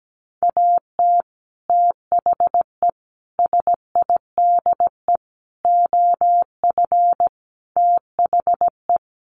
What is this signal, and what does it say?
Morse code, heard as a keyed tone, about 17 wpm, 700 Hz. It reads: AT THE SIDE OF THE
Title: Sets of 5 Words Encore2 17wpm